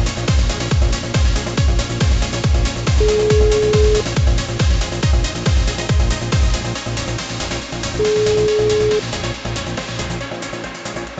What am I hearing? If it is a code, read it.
Ringback tone (Europe, 425 Hz), with noise.